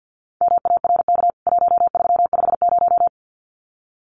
Morse code, heard as a keyed tone, about 50 wpm, 700 Hz. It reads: MARK 1350
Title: CWT Exchange 50wpm